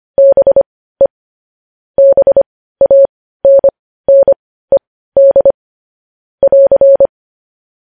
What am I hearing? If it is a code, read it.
Morse code, heard as a keyed tone, about 25 wpm, spaced as for 15 wpm, 550 Hz. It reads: BE BANNED <AR>
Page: news-headlines-in-morse-code-at-15-wpm